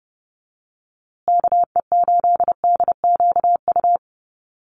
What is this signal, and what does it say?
Morse code, heard as a keyed tone, about 30 wpm, 700 Hz. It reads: KE8DQU
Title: USA Call Signs Repeated 3x 30wpm